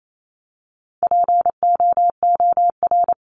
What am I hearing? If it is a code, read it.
Morse code, heard as a keyed tone, about 28 wpm, 700 Hz. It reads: POOR
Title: Top 751-1000 Words 28wpm